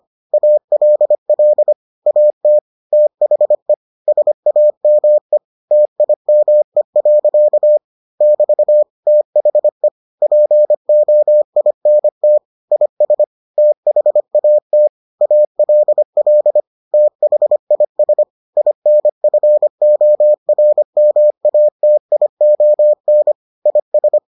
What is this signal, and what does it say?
Morse code, heard as a keyed tone, about 25 wpm, 600 Hz. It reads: ALL AT THE SAME TIME. <BT> THE POINT IS THAT ALL THIS INFORMATION IS